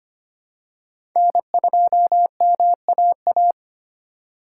Morse code, heard as a keyed tone, about 25 wpm, 700 Hz. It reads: N2MAA
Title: USA Call Signs - 4x Character Spacing - 25wpm